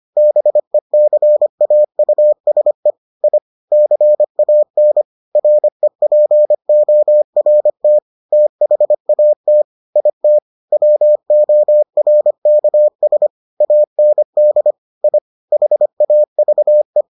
Morse code, heard as a keyed tone, about 25 wpm, 600 Hz. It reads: BECAUSE I CAN REPORT THAT IT WORKS AND I HAVE